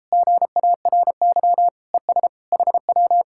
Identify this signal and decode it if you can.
Morse code, heard as a keyed tone, about 37 wpm, 700 Hz. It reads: GARY ES HW